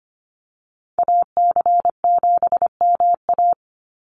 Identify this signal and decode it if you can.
Morse code, heard as a keyed tone, about 25 wpm, 700 Hz. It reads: AC7MA
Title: NA SOTA Activator Callsigns 25wpm